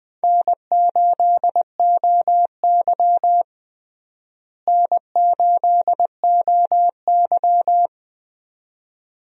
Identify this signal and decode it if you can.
Morse code, heard as a keyed tone, about 20 wpm, 700 Hz. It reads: N8OY N8OY
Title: USA Call Signs - 4 Characters - Repeated 2x - 20wpm